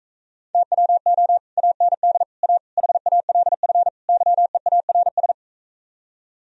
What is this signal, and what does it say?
Morse code, heard as a keyed tone, about 46 wpm, 700 Hz. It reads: TWO AND A HALF YEARS